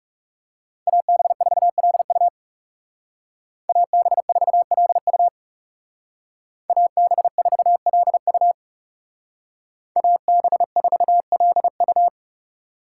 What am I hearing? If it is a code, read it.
Morse code, heard as a keyed tone, about 39 wpm, 700 Hz. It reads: AB4LU AB4LU AB4LU AB4LU